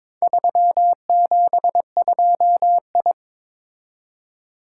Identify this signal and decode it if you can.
Morse code, heard as a keyed tone, about 22 wpm, 700 Hz. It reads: 372I
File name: practice / 22wpm / Four Letter-Number Combinations - 22wpm